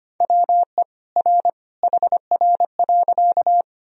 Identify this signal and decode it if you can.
Morse code, heard as a keyed tone, about 25 wpm, 700 Hz. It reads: WE R HR.
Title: Sentences from Top 200 Words and Abbreviations 25wpm